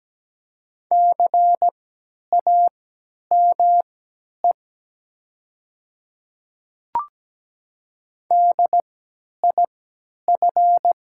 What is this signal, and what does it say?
Morse code, heard as a keyed tone, about 17 wpm, spaced as for 10 wpm, 700 Hz. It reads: CAME DIF
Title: 3x Character Spacing - Top 200 Words - 17wpm